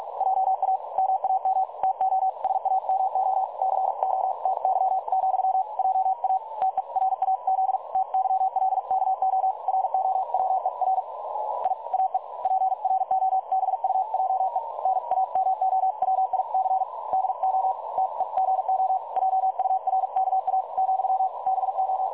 Morse code, heard as a keyed tone, about 46 wpm, 760 Hz. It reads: JA MAM TO SAMO CO TY, JA TERAZ TYLKO ROWER I AE PROBUJE NAMOWIC DO TEGO JURKA , BO
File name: dlazaawansowanych / nagraniazpasma / websdr recording 3565.0kHz